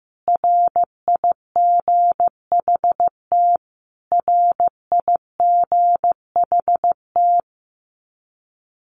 Morse code, heard as a keyed tone, about 15 wpm, 700 Hz. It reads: RIGHT RIGHT
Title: Top 300 Words Repeated 15wpm